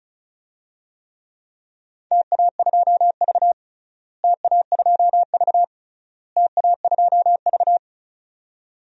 Morse code, heard as a keyed tone, about 36 wpm, 700 Hz. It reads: TA2V TA2V TA2V
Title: DXCC-100 Call Signs - 4 Characters - Repeated 3x - 35wpm